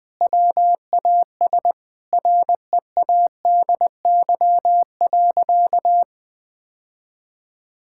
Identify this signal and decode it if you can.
Morse code, heard as a keyed tone, about 20 wpm, 700 Hz. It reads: WAS READY.
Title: Sentences with 3-Word Phrases - 20wpm